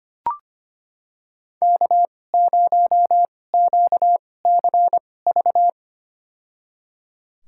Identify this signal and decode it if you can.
Morse code, heard as a keyed tone, about 25 wpm, spaced as for 18 wpm, 700 Hz. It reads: K0QCV
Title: USA Call Signs - 2x Character Spacing - 25wpm